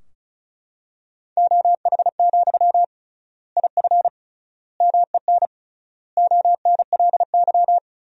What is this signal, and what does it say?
Morse code, heard as a keyed tone, about 36 wpm, 700 Hz. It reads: OH, IF MEN ONLY
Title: 3x Word-Spacing - Sentences from Top 500 Words - 35wpm